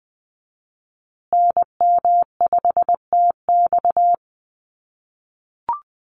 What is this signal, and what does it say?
Morse code, heard as a keyed tone, about 20 wpm, 700 Hz. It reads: NM5TX
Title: USA Call Signs Repeated 2x 20wpm